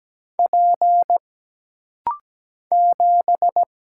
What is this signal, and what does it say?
Morse code, heard as a keyed tone, about 17 wpm, 700 Hz. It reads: P 7
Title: Single Letter-Number - Rapid-Fire 17wpm